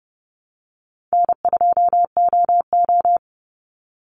Morse code, heard as a keyed tone, about 30 wpm, 700 Hz. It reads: N2OO
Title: Easy Call Signs Repeated 2x - 30wpm